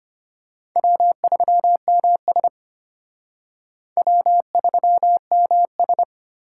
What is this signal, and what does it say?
Morse code, heard as a keyed tone, about 29 wpm, 700 Hz. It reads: W3MS W3MS
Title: Speed Racing - USA 1x2 Call Signs 25wpm